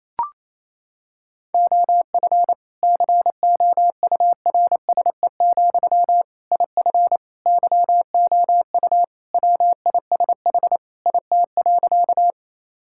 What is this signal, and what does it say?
Morse code, heard as a keyed tone, about 28 wpm, 700 Hz. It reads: OF COURSE, IF YOU WISH IT.
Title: Sentences from Top 1600 Words 28wpm